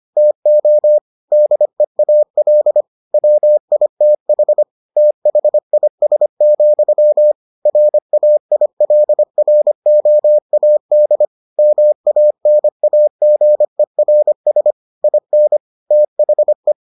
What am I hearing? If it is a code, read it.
Morse code, heard as a keyed tone, about 25 wpm, 600 Hz. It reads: TO DEAL WITH THIS, RAILROAD MANAGERS IN THE